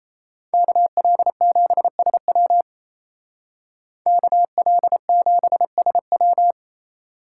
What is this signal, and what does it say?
Morse code, heard as a keyed tone, about 33 wpm, 700 Hz. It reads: KL7SW KL7SW
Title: Speed Racing - USA 2x2 Call Signs 28wpm